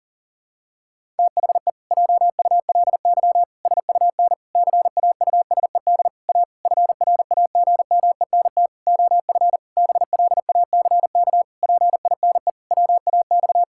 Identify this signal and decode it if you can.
Morse code, heard as a keyed tone, about 37 wpm, 700 Hz. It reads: THE JULY SUN CAUSED A FRAGMENT OF BLACK PINE WAX
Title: Pangram Challenge v2 - 40wpm